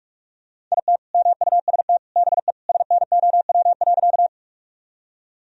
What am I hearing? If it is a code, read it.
Morse code, heard as a keyed tone, about 45 wpm, 700 Hz. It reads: IT MUST BE SNOW.
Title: Sentences from Top 1000 Words - Review Part 3 - 45wpm